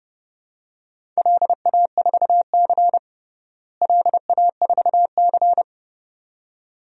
Morse code, heard as a keyed tone, about 30 wpm, 700 Hz. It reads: LA4C LA4C
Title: DXCC-100 Call Signs - 4 Characters - Repeated 2x - 30wpm